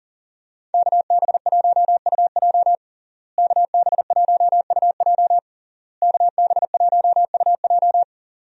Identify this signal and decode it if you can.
Morse code, heard as a keyed tone, about 40 wpm, 700 Hz. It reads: KB1UJ KB1UJ KB1UJ
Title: USA Call Signs - 5 Characters - Repeated 3x - 40wpm